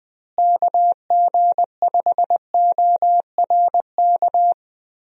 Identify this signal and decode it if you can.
Morse code, heard as a keyed tone, about 20 wpm, 700 Hz. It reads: KG5ORK
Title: USA Call Signs - 6 Characters - Repeated 2x - 20wpm